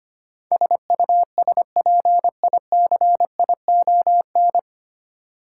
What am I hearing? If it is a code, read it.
Morse code, heard as a keyed tone, about 25 wpm, 700 Hz. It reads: SUSPICION
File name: Nine Letter Words - 25wpm